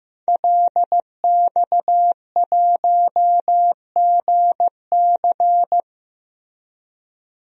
Morse code, heard as a keyed tone, about 15 wpm, 700 Hz. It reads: LX1GC
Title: DXCC-100 Call Signs - 5 Characters - 15wpm